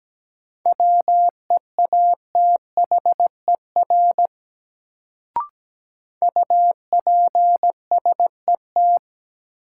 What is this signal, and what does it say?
Morse code, heard as a keyed tone, about 17 wpm, 700 Hz. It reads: WEATHER UPSET
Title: Sentences from Top 800 Words - New Word Drill - 17wpm